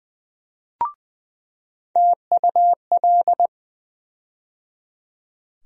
Morse code, heard as a keyed tone, about 20 wpm, 700 Hz. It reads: TUL
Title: USA Call Sign Postfixes - 3 Letters 20wpm